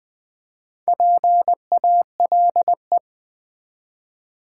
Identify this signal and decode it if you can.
Morse code, heard as a keyed tone, about 20 wpm, 700 Hz. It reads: PALE
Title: Build A Word - 1 Letter At A Time - 4-Letter Words - 20wpm